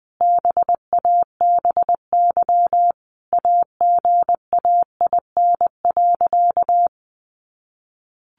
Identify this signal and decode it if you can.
Morse code, heard as a keyed tone, about 20 wpm, 700 Hz. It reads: BABY AGAIN.
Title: Sentences from Top 1300 Words 20wpm